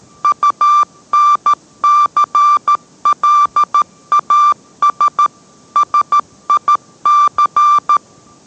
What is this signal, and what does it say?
Morse code, heard as a keyed tone, about 13 wpm, 1180 Hz. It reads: UNCLAS SIC